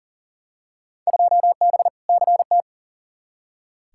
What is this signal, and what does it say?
Morse code, heard as a keyed tone, about 37 wpm, 700 Hz. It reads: 2B CT
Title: Field Day Exchanges Encore - 40wpm